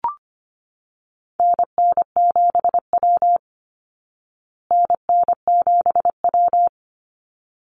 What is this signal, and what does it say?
Morse code, heard as a keyed tone, about 25 wpm, 700 Hz. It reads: NN7W NN7W
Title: Call Signs Repeated 3x - 25wpm